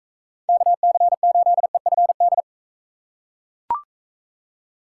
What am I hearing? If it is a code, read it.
Morse code, heard as a keyed tone, about 44 wpm, 700 Hz. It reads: KC8EFD